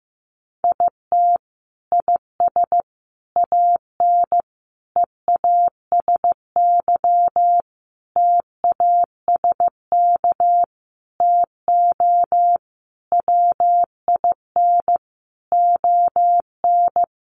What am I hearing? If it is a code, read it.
Morse code, heard as a keyed tone, about 15 wpm, 700 Hz. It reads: IT IS AN EASY TASK TO WIN ON
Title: Sentences from Top 1500 Words 15wpm